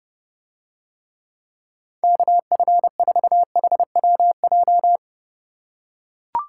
Morse code, heard as a keyed tone, about 32 wpm, 700 Hz. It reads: KF4HWJ